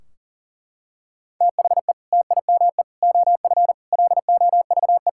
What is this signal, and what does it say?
Morse code, heard as a keyed tone, about 40 wpm, 700 Hz. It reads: THE TIME OF LOVE
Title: Sentences from Top 1600 Words 40wpm